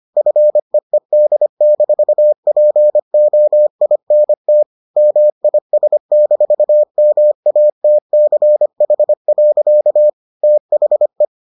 Morse code, heard as a keyed tone, about 25 wpm, 600 Hz. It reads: FEED-POINT MIS-MATCH. THE